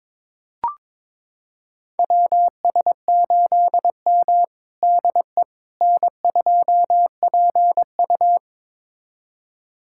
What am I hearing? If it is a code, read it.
Morse code, heard as a keyed tone, about 22 wpm, 700 Hz. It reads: WS8M DE N2PU
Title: USA Callsign DE Callsign - 4 Characters - 22wpm